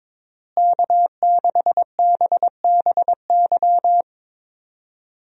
Morse code, heard as a keyed tone, about 22 wpm, 700 Hz. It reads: K6BBY
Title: USA Call Signs - 5 Characters - Repeated 2x - 22wpm